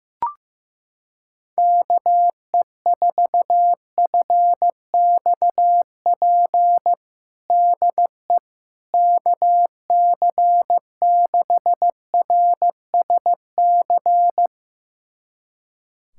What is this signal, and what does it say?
Morse code, heard as a keyed tone, about 15 wpm, 700 Hz. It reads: KE4FXP DE KC6RSC
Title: USA Callsign DE Callsign - 6 Characters - 15wpm